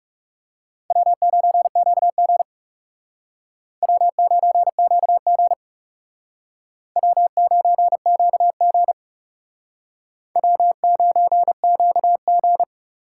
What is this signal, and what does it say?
Morse code, heard as a keyed tone, about 39 wpm, 700 Hz. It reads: W9QG W9QG W9QG W9QG